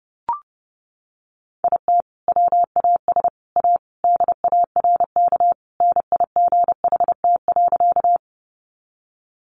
Morse code, heard as a keyed tone, about 30 wpm, 700 Hz. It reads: IT WAS A DARK NIGHT.
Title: Sentences from Top 500 Words 30wpm